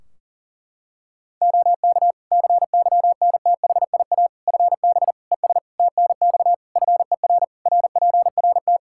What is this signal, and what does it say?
Morse code, heard as a keyed tone, about 40 wpm, 700 Hz. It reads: OK CYNTHIA FB ES TNX FER RPRT
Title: K1USN SST QSO Practice 40wpm